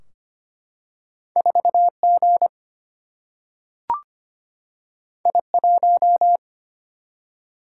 Morse code, heard as a keyed tone, about 25 wpm, 700 Hz. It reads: 4G I1
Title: Single-Letter Single-Number 25wpm